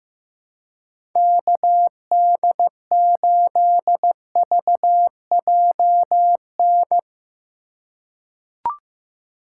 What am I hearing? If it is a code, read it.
Morse code, heard as a keyed tone, about 15 wpm, 700 Hz. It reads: KD8VJN
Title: USA Call Signs - 3x Character Spacing - 15wpm